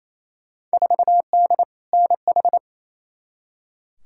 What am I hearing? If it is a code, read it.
Morse code, heard as a keyed tone, about 28 wpm, 700 Hz. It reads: 4D NH
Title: Field Day Exchanges Encore - 28wpm